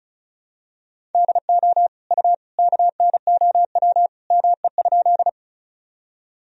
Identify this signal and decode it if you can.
Morse code, heard as a keyed tone, about 34 wpm, 700 Hz. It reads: DO U KNOW ME?
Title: Sentences from Top 200 Words and Abbreviations 35wpm